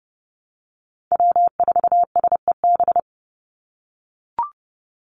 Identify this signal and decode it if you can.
Morse code, heard as a keyed tone, about 30 wpm, 700 Hz. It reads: W4SEB
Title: USA Call Signs 30wpm